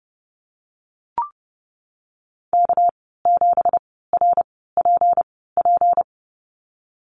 Morse code, heard as a keyed tone, about 32 wpm, spaced as for 17 wpm, 700 Hz. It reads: K7RPP
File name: Lesson 37g - Letter x - Call Signs 30wpm